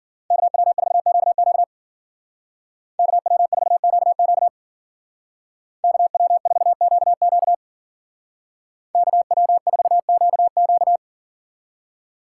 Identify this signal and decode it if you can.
Morse code, heard as a keyed tone, about 48 wpm, 700 Hz. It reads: KW4QQ KW4QQ KW4QQ KW4QQ
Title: Speed Racing - USA Call Signs 40wpm